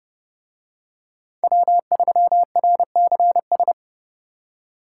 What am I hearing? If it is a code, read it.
Morse code, heard as a keyed tone, about 27 wpm, 700 Hz. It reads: W3RCS